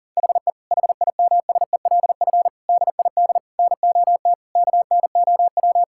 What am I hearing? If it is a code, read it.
Morse code, heard as a keyed tone, about 40 wpm, 700 Hz. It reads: HE HIMSELF DID NOT KNOW
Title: Sentences from Top 1600 Words 40wpm